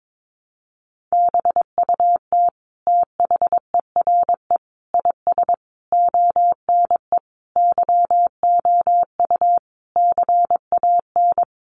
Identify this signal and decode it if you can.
Morse code, heard as a keyed tone, about 22 wpm, 700 Hz. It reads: BUT THERE IS ONE YOU CAN
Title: Sentences from Top 1200 Words 22wpm